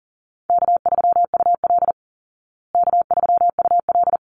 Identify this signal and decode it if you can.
Morse code, heard as a keyed tone, about 40 wpm, 700 Hz. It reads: K3UL K3UL
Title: Easy Call Signs Repeated 2x - 40wpm